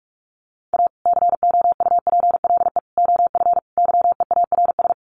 Morse code, heard as a keyed tone, about 44 wpm, 700 Hz. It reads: A COUPLE OF YEARS
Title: Sets of 4 Words 45wpm